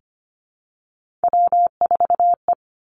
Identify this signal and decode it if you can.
Morse code, heard as a keyed tone, about 25 wpm, 700 Hz. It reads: W4E
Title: Three Challenging Letter-Number Combinations 25wpm